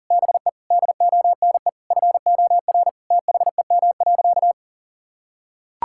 Morse code, heard as a keyed tone, about 37 wpm, 700 Hz. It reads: BE DONE FOR THEM.
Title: Sentences from Top 1600 Words 40wpm